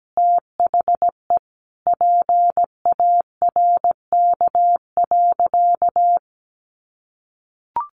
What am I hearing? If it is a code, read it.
Morse code, heard as a keyed tone, about 17 wpm, 700 Hz. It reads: THE PARK.
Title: Sentences from Top 1400 Words 17wpm